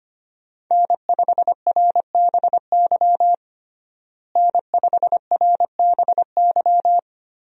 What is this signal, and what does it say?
Morse code, heard as a keyed tone, about 25 wpm, 700 Hz. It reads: N5RBY N5RBY
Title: USA Call Signs - 5 Characters - Repeated 3x - 25wpm